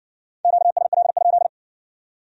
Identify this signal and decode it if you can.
Morse code, heard as a keyed tone, about 56 wpm, 700 Hz. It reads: QSL?